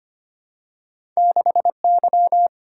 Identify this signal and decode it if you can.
Morse code, heard as a keyed tone, about 25 wpm, 700 Hz. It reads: 6Y
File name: Two Letter-Number Combinations - 25wpm